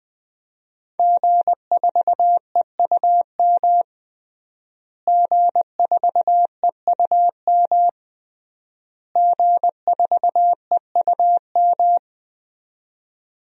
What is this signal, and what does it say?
Morse code, heard as a keyed tone, about 21 wpm, 700 Hz. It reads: G4EUM G4EUM G4EUM
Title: DXCC-100 Call Signs - 5 Characters - Repeated 3x - 20wpm